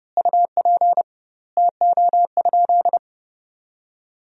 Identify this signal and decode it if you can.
Morse code, heard as a keyed tone, about 29 wpm, 700 Hz. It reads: UP TO?
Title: 2x Word-Spacing - Sentences from Top 100 Words - 30wpm